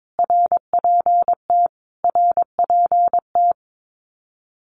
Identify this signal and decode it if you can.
Morse code, heard as a keyed tone, about 22 wpm, 700 Hz. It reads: RPT RPT
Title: Abbreviations Repeated 3x - 22wpm